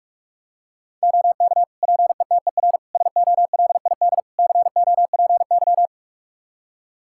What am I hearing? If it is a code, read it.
Morse code, heard as a keyed tone, about 44 wpm, 700 Hz. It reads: OK PETER SOLID COPY
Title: K1USN SST QSO Practice 45wpm